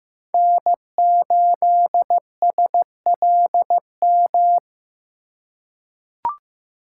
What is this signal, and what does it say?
Morse code, heard as a keyed tone, about 15 wpm, 700 Hz. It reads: N8SLM
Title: USA Call Signs - 5 Characters - 2x Character Spacing - 15wpm